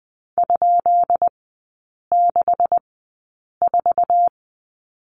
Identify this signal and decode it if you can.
Morse code, heard as a keyed tone, about 20 wpm, spaced as for 8 wpm, 700 Hz. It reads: ?64
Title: 133 Lesson 22e - Question Mark - Three Characters 20wpm